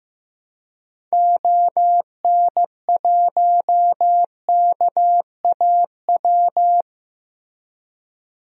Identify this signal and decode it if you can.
Morse code, heard as a keyed tone, about 15 wpm, 700 Hz. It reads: ON1KAW
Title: DXCC-100 Call Signs - 6 Characters - Repeated 3x - 15wpm